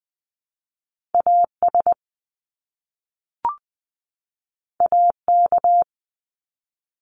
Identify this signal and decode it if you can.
Morse code, heard as a keyed tone, about 20 wpm, 700 Hz. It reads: AS AK
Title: Top Two Letter Combinations in Words 20wpm